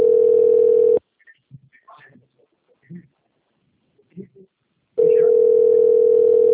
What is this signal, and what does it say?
Ringback tone (North America), 440 + 480 Hz, with speech.
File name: out- - - - - .62511